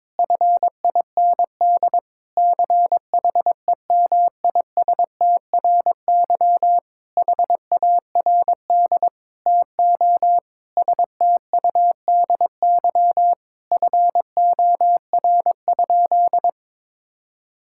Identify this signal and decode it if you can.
Morse code, heard as a keyed tone, about 21 wpm, 700 Hz. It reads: FIND CHEMISTRY HARD TO STUDY FOR?
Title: Sentences from Top 1000 Words - Review Part 6 - 22wpm